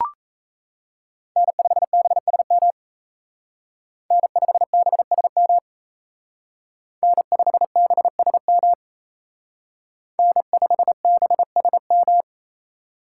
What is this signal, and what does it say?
Morse code, heard as a keyed tone, about 34 wpm, 700 Hz. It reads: N5BSM N5BSM N5BSM N5BSM
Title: Speed Racing - USA 1x3 Call Signs 28wpm